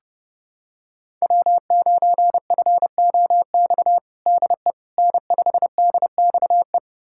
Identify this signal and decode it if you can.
Morse code, heard as a keyed tone, about 30 wpm, 700 Hz. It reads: W9FOX DE N5DXE
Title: USA Callsign DE Callsign - 5 Characters - 30wpm